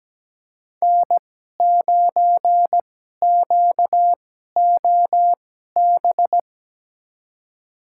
Morse code, heard as a keyed tone, about 17 wpm, spaced as for 12 wpm, 700 Hz. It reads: N9QOB
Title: USA Call Signs - 2x Character Spacing - 17wpm